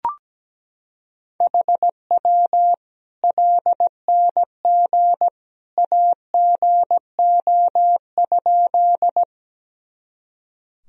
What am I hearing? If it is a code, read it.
Morse code, heard as a keyed tone, about 17 wpm, 700 Hz. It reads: HW LNG AGO?
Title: Sentences from Top 500 Words and Abbreviations 17wpm